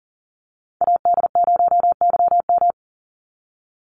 Morse code, heard as a keyed tone, about 40 wpm, 700 Hz. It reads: AD0YM
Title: NA SOTA Chaser Callsigns 40wpm